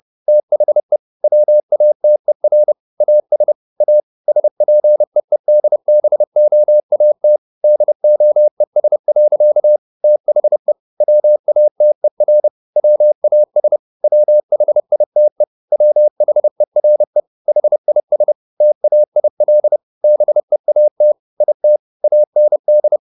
Morse code, heard as a keyed tone, about 30 wpm, 600 Hz. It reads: THE WATER AS A SPEEDBOAT DOES. THE WATER WAS WHITE WHERE HIS TAIL BEAT IT AND